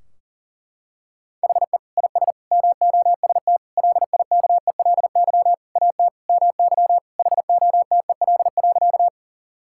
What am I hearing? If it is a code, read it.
Morse code, heard as a keyed tone, about 37 wpm, 700 Hz. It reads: HE IS MOST LIKELY AT MY HOTEL.
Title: Sentences from Top 1600 Words 40wpm